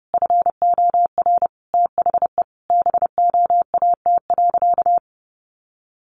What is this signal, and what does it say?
Morse code, heard as a keyed tone, about 30 wpm, 700 Hz. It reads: FOR THE BOAT.
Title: Sentences from Top 1100 Words 30wpm